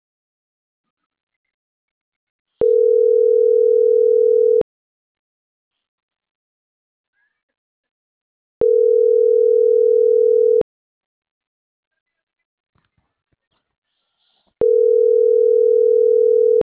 Ringback tone (North America), 440 + 480 Hz.